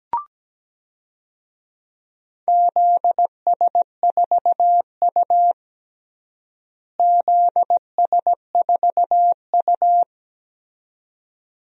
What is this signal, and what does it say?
Morse code, heard as a keyed tone, about 17 wpm, 700 Hz. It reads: ZS4U ZS4U
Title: DXCC-100 Call Signs - 4 Characters - Repeated 2x - 17wpm